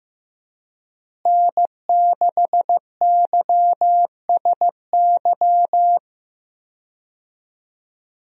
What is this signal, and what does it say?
Morse code, heard as a keyed tone, about 15 wpm, 700 Hz. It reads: N6YSY
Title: USA Call Signs - 5 Characters - Repeated 2x - 15wpm